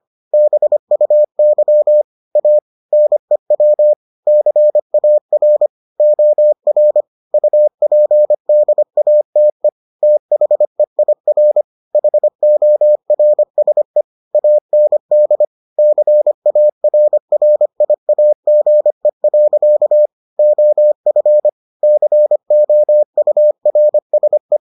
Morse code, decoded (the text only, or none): BUY A NEW CAR OR UPDATE THEIR HORSE AND CARRIAGE. OF COURSE